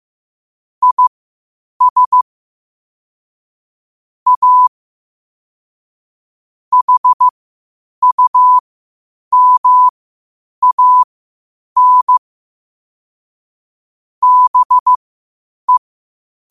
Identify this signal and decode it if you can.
Morse code, heard as a keyed tone, about 15 wpm, spaced as for 8 wpm, 1000 Hz. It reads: IS A HUMAN BE
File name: qotdfiles / 8wpm